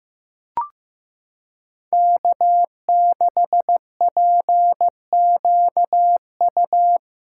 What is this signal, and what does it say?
Morse code, heard as a keyed tone, about 15 wpm, 700 Hz. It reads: K6PQU
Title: USA Callsign DE Callsign - 5 Characters - 15wpm